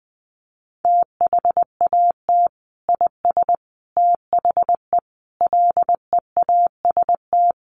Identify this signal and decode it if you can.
Morse code, heard as a keyed tone, about 20 wpm, 700 Hz. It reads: THAT IS THE LEAST
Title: Sentences from Top 700 Words 20wpm